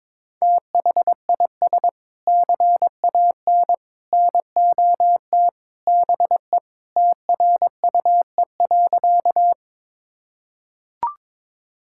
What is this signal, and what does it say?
Morse code, heard as a keyed tone, about 21 wpm, 700 Hz. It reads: THIS CAN NOT BE TRUE.
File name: practice / 22wpm / Sentences with 3-Word Phrases - 22wpm